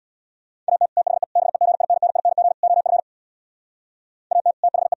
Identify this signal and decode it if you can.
Morse code, heard as a keyed tone, about 64 wpm, 700 Hz. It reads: AT THE BEGINNING OF AT THE